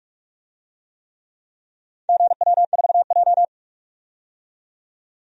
Morse code, heard as a keyed tone, about 44 wpm, 700 Hz. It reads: GW4J
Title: DXCC-100 Call Signs - 4 Characters - 45wpm